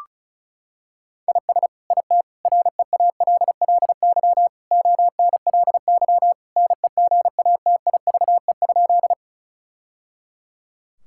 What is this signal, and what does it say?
Morse code, heard as a keyed tone, about 34 wpm, 700 Hz. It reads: IS IT REALLY ONLY NEGATIVE?
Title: Sentences from Top 1000 Words - Review Part 6 - 35wpm